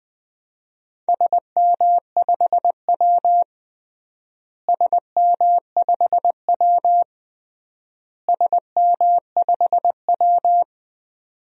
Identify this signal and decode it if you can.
Morse code, heard as a keyed tone, about 20 wpm, 700 Hz. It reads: SM5W SM5W SM5W
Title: DXCC-100 Call Signs - 4 Characters - Repeated 3x - 20wpm